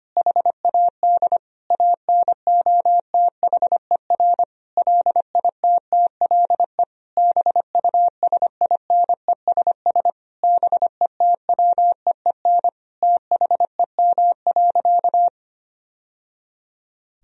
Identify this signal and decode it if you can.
Morse code, heard as a keyed tone, about 25 wpm, 700 Hz. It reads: HAD ANOTHER LITTLE BUSINESS BETWEEN THEM.
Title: Sentences from Top 1000 Words - Review Part 1 - 25wpm